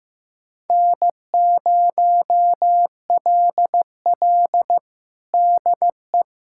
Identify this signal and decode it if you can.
Morse code, heard as a keyed tone, about 15 wpm, 700 Hz. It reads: N0LL DE
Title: USA Callsign DE Callsign 15wpm